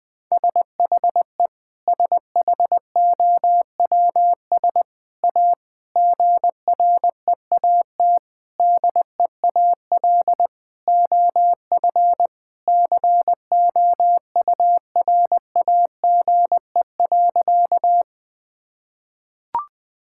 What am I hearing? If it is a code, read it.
Morse code, heard as a keyed tone, about 20 wpm, 700 Hz. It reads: SHE SHOWS A GREAT DEAL OF COURAGE.
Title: Sentences with 4-Word Phrases - 20wpm